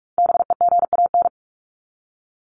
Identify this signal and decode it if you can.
Morse code, heard as a keyed tone, about 44 wpm, 700 Hz. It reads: BEGAN
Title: Top 500 Words 45wpm